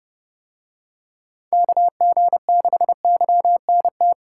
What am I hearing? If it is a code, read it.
Morse code, heard as a keyed tone, about 30 wpm, 700 Hz. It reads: KG6YNT